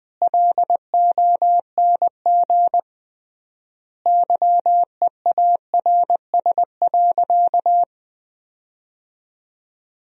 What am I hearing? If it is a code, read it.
Morse code, heard as a keyed tone, about 20 wpm, 700 Hz. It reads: LONG YEARS.
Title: 3x Word-Spacing - Sentences from Top 200 Words - 20wpm